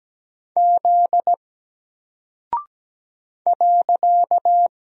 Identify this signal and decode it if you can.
Morse code, heard as a keyed tone, about 17 wpm, 700 Hz. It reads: Z .
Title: Single Character (Letter:Number:Punctuation:Prosign) 17wpm